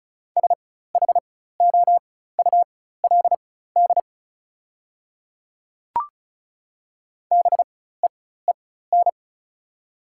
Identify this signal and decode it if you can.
Morse code, heard as a keyed tone, about 34 wpm, spaced as for 16 wpm, 700 Hz. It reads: SHOULD BEEN
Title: 4x Character Spacing - Top 300 Words - 35wpm